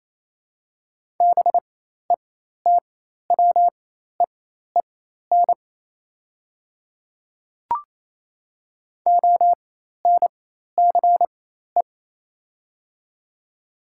Morse code, heard as a keyed tone, about 28 wpm, spaced as for 13 wpm, 690 Hz. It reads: BETWEEN ONCE
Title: 4x Character Spacing - Top 300 Words - 28wpm